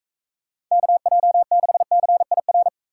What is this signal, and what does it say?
Morse code, heard as a keyed tone, about 40 wpm, 700 Hz. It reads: KJ6CIR